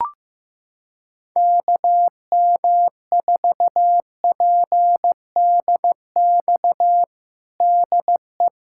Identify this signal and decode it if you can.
Morse code, heard as a keyed tone, about 16 wpm, 700 Hz. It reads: KM4PDX DE
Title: USA Callsign DE Callsign - 6 Characters - 15wpm